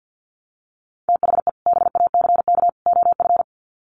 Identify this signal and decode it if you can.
Morse code, heard as a keyed tone, about 50 wpm, 700 Hz. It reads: THE BACK OF
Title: Sets of 3 Words 50wpm